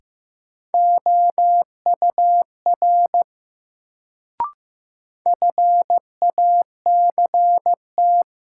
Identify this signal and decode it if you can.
Morse code, heard as a keyed tone, about 15 wpm, 700 Hz. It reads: OUR FACT
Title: Sets of 2 Words - New Word Drill - 15wpm